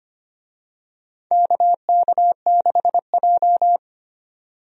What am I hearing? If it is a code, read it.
Morse code, heard as a keyed tone, about 25 wpm, 700 Hz. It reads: KK6J
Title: USA Call Signs - 4 Characters - 25wpm